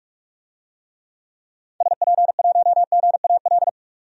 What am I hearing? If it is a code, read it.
Morse code, heard as a keyed tone, about 47 wpm, 700 Hz. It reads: SP1GAL
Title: DXCC-100 Call Signs - 6 Characters - 45wpm